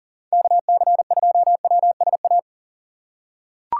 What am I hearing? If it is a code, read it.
Morse code, heard as a keyed tone, about 40 wpm, 700 Hz. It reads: KC2WSA
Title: USA Call Signs - 2x Character Spacing - 40wpm